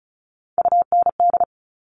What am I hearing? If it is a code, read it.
Morse code, heard as a keyed tone, about 36 wpm, 700 Hz. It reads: UND
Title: Top Three Letter Combinations in Words 35wpm